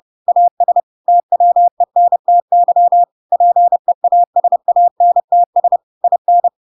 Morse code, heard as a keyed tone, about 30 wpm, 700 Hz. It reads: AS TWENTY PEASANTS IN
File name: Pinocchio / Pinocchio ch3 30 30 ik1hge